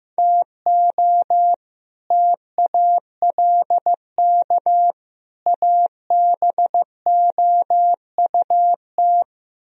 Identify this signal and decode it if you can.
Morse code, heard as a keyed tone, about 15 wpm, 700 Hz. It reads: TO TALK ABOUT
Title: Speed-Racing - Sets of 3 Words 15wpm